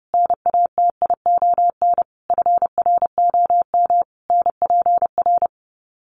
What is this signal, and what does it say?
Morse code, heard as a keyed tone, about 28 wpm, 700 Hz. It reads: NATION FROM NPR
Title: Sets of 5 Words 30wpm